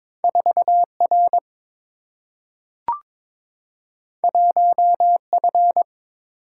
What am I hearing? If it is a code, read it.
Morse code, heard as a keyed tone, about 22 wpm, 700 Hz. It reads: 4 R 1 F
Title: Single-Letter Single-Number 22wpm